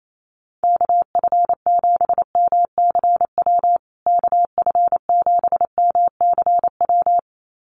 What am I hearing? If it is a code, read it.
Morse code, heard as a keyed tone, about 28 wpm, 700 Hz. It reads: KF7MCW KF7MCW
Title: USA Call Signs Repeated 2x 28wpm